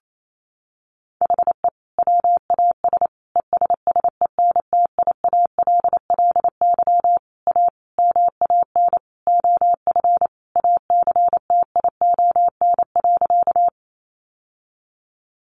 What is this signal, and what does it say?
Morse code, heard as a keyed tone, about 28 wpm, 700 Hz. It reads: HE WAS ESSENTIALLY A MAN OF ACTION.
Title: Sentences from Top 1300 Words 28wpm